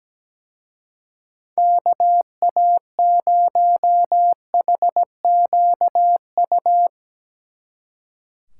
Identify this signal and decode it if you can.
Morse code, heard as a keyed tone, about 18 wpm, 700 Hz. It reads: KA0HQU
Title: Speed Racing - USA 2x3 Call Signs 17wpm